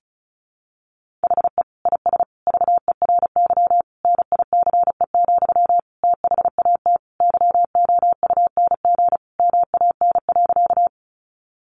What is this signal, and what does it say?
Morse code, heard as a keyed tone, about 34 wpm, 700 Hz. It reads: HE IS VERY NICE, THAT YOUNG MAN.
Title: Sentences from Top 1100 Words 35wpm